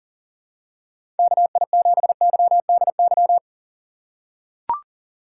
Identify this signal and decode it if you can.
Morse code, heard as a keyed tone, about 40 wpm, 700 Hz. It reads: KI7YDY
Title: USA Call Signs - 3x Character Spacing - 40wpm